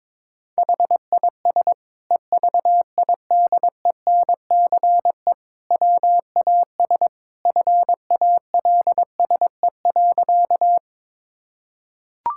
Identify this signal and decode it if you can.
Morse code, heard as a keyed tone, about 22 wpm, 700 Hz. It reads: HIS EVIDENCE WAS FALSE.
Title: Sentences from Top 1600 Words 22wpm